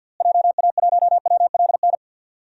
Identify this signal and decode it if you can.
Morse code, heard as a keyed tone, about 48 wpm, 700 Hz. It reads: JA1WLN
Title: DXCC-100 Call Signs - 6 Characters - Repeated 2x - 50wpm